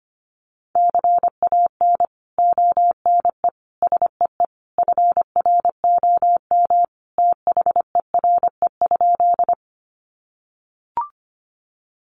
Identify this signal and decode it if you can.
Morse code, heard as a keyed tone, about 25 wpm, 700 Hz. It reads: CAN ONE SEE FROM THERE?
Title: Sentences from Top 100 Words 25wpm